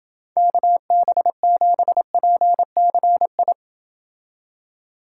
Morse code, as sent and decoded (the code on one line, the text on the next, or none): -.- -... --... .--. -.-. ..
KB7PCI